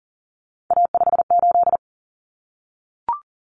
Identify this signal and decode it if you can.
Morse code, heard as a keyed tone, about 40 wpm, 700 Hz. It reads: A58
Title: Common Cut Numbers - 3-Digit 40wpm